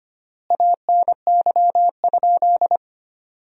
Morse code, heard as a keyed tone, about 25 wpm, 700 Hz. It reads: ANY?
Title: 3x Word-Spacing - Sentences from Top 300 Words - 25wpm